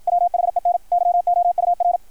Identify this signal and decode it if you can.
Morse code, heard as a keyed tone, about 53 wpm, 690 Hz. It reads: WHEN YOUR
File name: archived / CRSnet